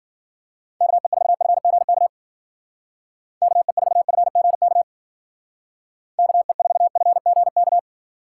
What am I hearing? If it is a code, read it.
Morse code, heard as a keyed tone, about 49 wpm, 700 Hz. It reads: KE4FGK KE4FGK KE4FGK
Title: Speed Racing - USA 2x3 Call Signs 40wpm